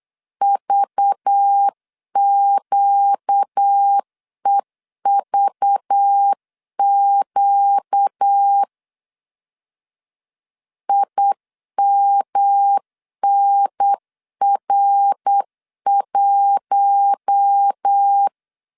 Morse code, heard as a keyed tone, about 8 wpm, 800 Hz. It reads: VQEVQ IMNR1
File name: CW / DET / 1 6 1500C S C 8wpm 5219